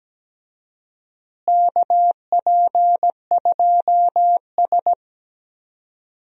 Morse code, heard as a keyed tone, about 17 wpm, 700 Hz. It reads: KP2S
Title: DXCC-100 Call Signs - 4 Characters - Repeated 3x - 17wpm